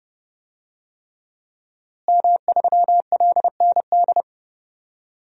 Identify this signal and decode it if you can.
Morse code, heard as a keyed tone, about 30 wpm, 700 Hz. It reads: M3LND
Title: DXCC-100 Call Signs - 5 Characters - Repeated 3x - 30wpm